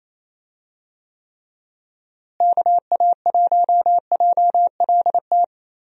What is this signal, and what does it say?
Morse code, heard as a keyed tone, about 28 wpm, 700 Hz. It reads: KA1JLT
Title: USA Call Signs - 6 Characters - Repeated 2x - 28wpm